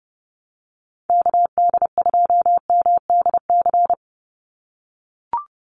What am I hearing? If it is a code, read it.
Morse code, heard as a keyed tone, about 30 wpm, 700 Hz. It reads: KD2MDC
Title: USA Call Signs 30wpm